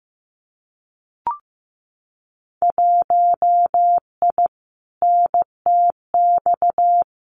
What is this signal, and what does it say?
Morse code, heard as a keyed tone, about 15 wpm, 700 Hz. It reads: 1I NTX
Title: Winter Field Day Exchanges Encore 15wpm